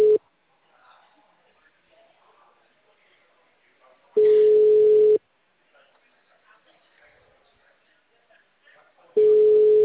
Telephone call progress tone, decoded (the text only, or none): ringback tone